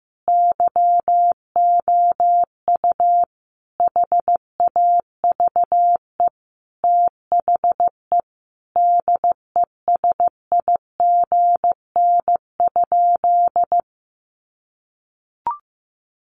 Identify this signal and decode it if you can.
Morse code, heard as a keyed tone, about 15 wpm, 700 Hz. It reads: YOU HAVE THE DESIGN?
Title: Sentences from Top 800 Words 15wpm